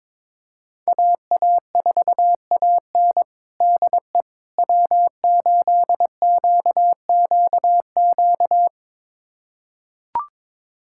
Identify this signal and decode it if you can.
Morse code, heard as a keyed tone, about 21 wpm, 700 Hz. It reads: AA4AN DE W8QQQ
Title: USA Callsign DE Callsign 22wpm